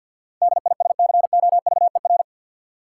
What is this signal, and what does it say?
Morse code, heard as a keyed tone, about 52 wpm, 700 Hz. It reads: DISCOVER
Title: Sentences from Top 1100 Words - New Word Drill - 50wpm